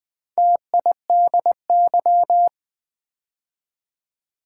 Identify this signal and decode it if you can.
Morse code, heard as a keyed tone, about 20 wpm, 690 Hz. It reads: TIDY
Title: Build A Word - 2 Letters At A Time - 4-Letter Words - 20wpm